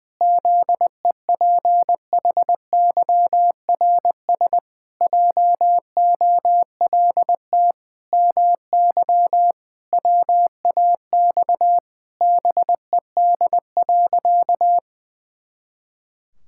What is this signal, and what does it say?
Morse code, heard as a keyed tone, about 20 wpm, 700 Hz. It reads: ZEPHYRS JOLT MY WAX BED.
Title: Pangram Challenge v2 - 20wpm